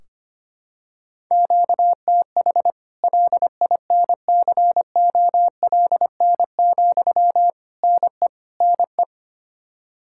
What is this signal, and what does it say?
Morse code, heard as a keyed tone, about 25 wpm, 700 Hz. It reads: QTH LINCOLN, NE NE
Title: Practice QSO - Exchange 1 - Beginning - 25wpm